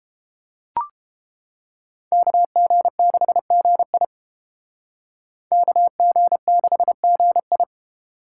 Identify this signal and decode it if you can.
Morse code, heard as a keyed tone, about 29 wpm, 700 Hz. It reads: KG6GI KG6GI